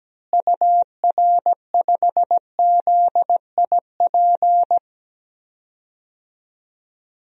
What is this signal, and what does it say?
Morse code, heard as a keyed tone, about 17 wpm, 700 Hz. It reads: UR5ZIP
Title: DXCC-100 Call Signs - 6 Characters - Repeated 3x - 17wpm